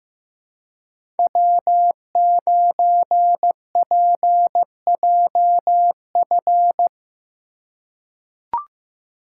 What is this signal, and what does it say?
Morse code, heard as a keyed tone, about 15 wpm, 700 Hz. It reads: W9PJF
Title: USA Call Signs - 5 Characters - 3x Character Spacing - 15wpm